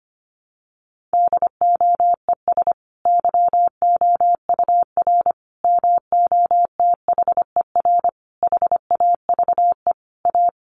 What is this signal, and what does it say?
Morse code, heard as a keyed tone, about 25 wpm, 700 Hz. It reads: DOES YOUR MOTHER HAVE A
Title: Sentences from Top 1400 Words 25wpm